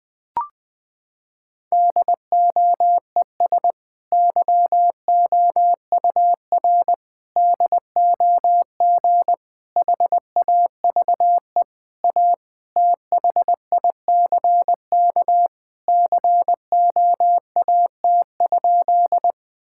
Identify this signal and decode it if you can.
Morse code, heard as a keyed tone, about 20 wpm, 700 Hz. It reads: DOES YOUR DOG HAVE A THICK COAT?
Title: Sentences from Top 1600 Words 20wpm